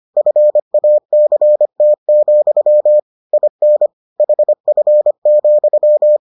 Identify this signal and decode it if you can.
Morse code, heard as a keyed tone, about 25 wpm, 600 Hz. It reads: FACT, IN HF,